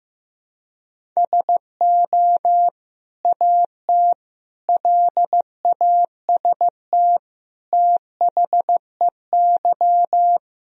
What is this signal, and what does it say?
Morse code, heard as a keyed tone, about 15 wpm, 700 Hz. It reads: SO AT LAST THEY
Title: Sentences from Top 1000 Words - Review Part 1 - 15wpm